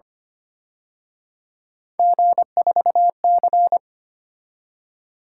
Morse code, heard as a keyed tone, about 29 wpm, 700 Hz. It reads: G4C
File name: Speed-Racing - Three Letter-Number Combinations - 25wpm